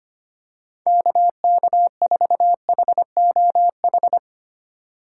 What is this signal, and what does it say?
Morse code, heard as a keyed tone, about 25 wpm, 700 Hz. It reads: KK4HOH
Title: USA Call Signs - 6 Characters - 3x Character Spacing - 25wpm